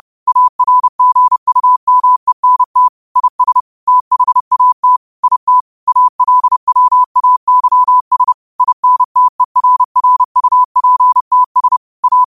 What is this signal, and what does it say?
Morse code, heard as a keyed tone, about 30 wpm, 1000 Hz. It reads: ARGUMENT IS THAT IT ALWAYS INTERRUPTS A
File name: qotdfiles / 30wpm